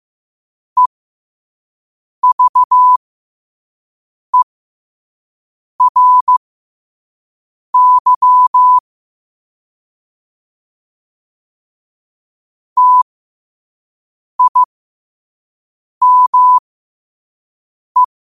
Morse code, heard as a keyed tone, about 15 wpm, spaced as for 5 wpm, 1000 Hz. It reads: EVERY TIME
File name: 5wpm